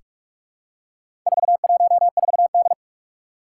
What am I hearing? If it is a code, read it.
Morse code, heard as a keyed tone, about 44 wpm, 700 Hz. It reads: 414D
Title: Four Challenging Letter-Number Combinations 45wpm